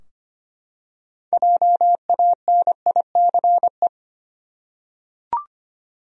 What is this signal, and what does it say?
Morse code, heard as a keyed tone, about 25 wpm, 700 Hz. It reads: JANICE
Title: USA Common Names - 25wpm